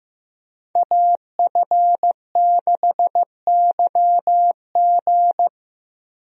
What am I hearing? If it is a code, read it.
Morse code, heard as a keyed tone, about 15 wpm, 700 Hz. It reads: AF6YG
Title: USA Call Signs - 5 Characters - 15wpm